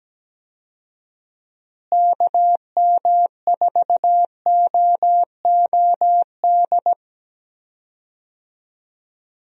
Morse code, heard as a keyed tone, about 17 wpm, 700 Hz. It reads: KM4OOD
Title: USA Call Signs - 6 Characters - Repeated 3x - 17wpm